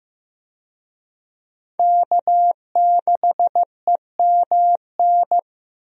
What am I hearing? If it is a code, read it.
Morse code, heard as a keyed tone, about 15 wpm, 700 Hz. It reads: K6EMN
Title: USA Call Signs - 5 Characters - Repeated 2x - 15wpm